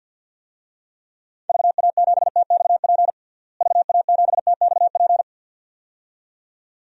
Morse code, heard as a keyed tone, about 51 wpm, 700 Hz. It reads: VA7TXP VA7TXP
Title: DXCC-100 Call Signs - 6 Characters - Repeated 2x - 50wpm